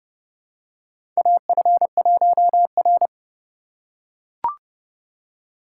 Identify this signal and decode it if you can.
Morse code, heard as a keyed tone, about 30 wpm, 700 Hz. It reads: AF1R
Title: USA Call Signs - 4 Characters - 4x Character Spacing - 30wpm